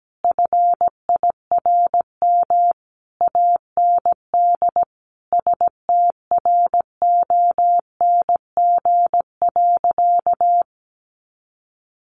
Morse code, heard as a keyed tone, about 17 wpm, 700 Hz. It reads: FIRM AND STRONG.
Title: Sentences from Top 1400 Words 17wpm